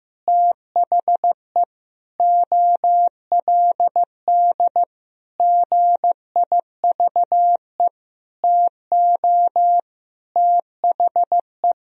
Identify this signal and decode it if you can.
Morse code, heard as a keyed tone, about 16 wpm, 700 Hz. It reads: THE OLD GIVE TO THE
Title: Sentences from Top 1000 Words - Review Part 1 - 15wpm